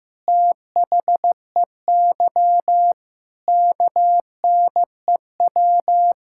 Morse code, heard as a keyed tone, about 15 wpm, 700 Hz. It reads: THEY KNEW
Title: Sentences from Top 1000 Words - Review Part 6 - 15wpm